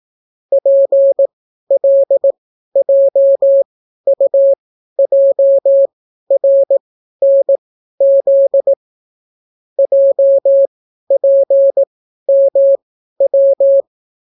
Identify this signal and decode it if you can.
Morse code, heard as a keyed tone, about 18 wpm, spaced as for 12 wpm, 550 Hz. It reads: PLJUJRNZ JPMW